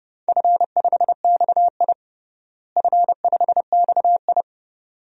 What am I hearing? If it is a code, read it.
Morse code, heard as a keyed tone, about 30 wpm, 700 Hz. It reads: F5XI F5XI
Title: DXCC-100 Call Signs - 4 Characters - Repeated 2x - 30wpm